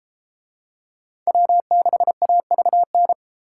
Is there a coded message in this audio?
Yes: Morse code from 1.3 to 3.1 s, heard as a keyed tone, about 32 wpm, 700 Hz. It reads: W6AVN